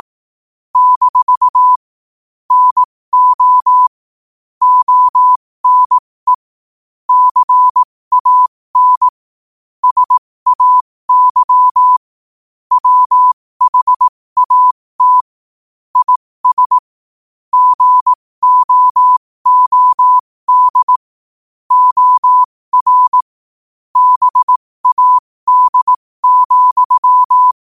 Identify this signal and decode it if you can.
Morse code, heard as a keyed tone, about 18 wpm, spaced as for 15 wpm, 1000 Hz. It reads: - NO ONE CAN SAY WHAT IS GOOD OR BAD,
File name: qotdfiles / 15wpm